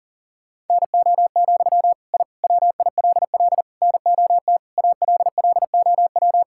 Morse code, heard as a keyed tone, about 40 wpm, 700 Hz. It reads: NO, I WILL NOT ALLOW
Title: Sentences from Top 1000 Words - Review Part 5 - 40wpm